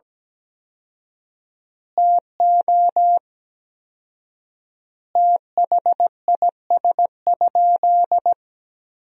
Morse code, heard as a keyed tone, about 17 wpm, 700 Hz. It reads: TO THIS?
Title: 4x Word-Spacing - Sentences from Top 100 Words - 17wpm